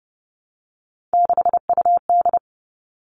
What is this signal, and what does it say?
Morse code, heard as a keyed tone, about 30 wpm, 700 Hz. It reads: 6UD
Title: Three Challenging Letter-Number Combinations 30wpm